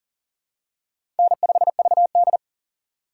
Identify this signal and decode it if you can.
Morse code, heard as a keyed tone, about 37 wpm, 700 Hz. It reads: N5VD
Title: USA Call Signs - 2x Character Spacing - 40wpm